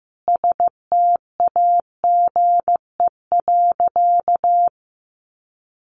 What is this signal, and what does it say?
Morse code, heard as a keyed tone, about 15 wpm, 700 Hz. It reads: STAGE.
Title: Sentences from Top 1200 Words 15wpm